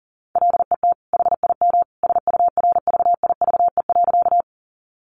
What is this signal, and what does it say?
Morse code, heard as a keyed tone, about 37 wpm, 700 Hz. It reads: LET HIM SURVIVE.
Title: Sentences from Top 1300 Words 40wpm